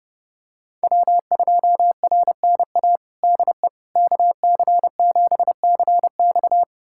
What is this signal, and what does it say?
Morse code, heard as a keyed tone, about 30 wpm, 700 Hz. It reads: W2RNA DE KC7CX
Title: USA Callsign DE Callsign - 5 Characters - 30wpm